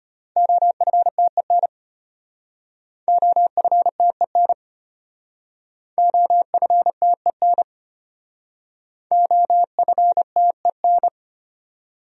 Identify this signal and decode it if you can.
Morse code, heard as a keyed tone, about 33 wpm, 700 Hz. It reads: OFTEN OFTEN OFTEN OFTEN